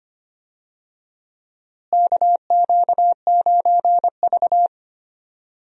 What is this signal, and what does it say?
Morse code, heard as a keyed tone, about 25 wpm, 700 Hz. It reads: KQ9V
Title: USA Call Signs - 4 Characters - Repeated 3x - 25wpm